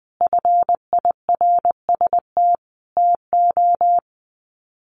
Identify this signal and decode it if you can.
Morse code, heard as a keyed tone, about 20 wpm, 700 Hz. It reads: FIRST TO
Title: Sets of 5 Words Encore 20wpm